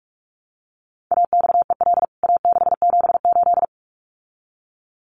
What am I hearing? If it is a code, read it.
Morse code, heard as a keyed tone, about 44 wpm, 700 Hz. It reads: AXEL A678
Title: CWT Exchange 45wpm